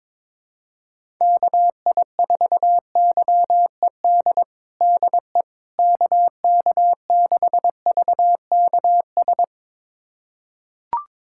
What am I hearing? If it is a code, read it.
Morse code, heard as a keyed tone, about 22 wpm, 700 Hz. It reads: KI4YED DE KK6VKS
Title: USA Callsign DE Callsign - 6 Characters - 22wpm